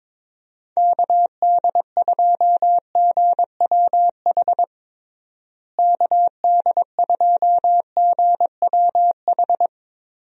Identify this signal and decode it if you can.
Morse code, heard as a keyed tone, about 21 wpm, 700 Hz. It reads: KD2GWH KD2GWH
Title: USA Call Signs - 6 Characters - Repeated 3x - 22wpm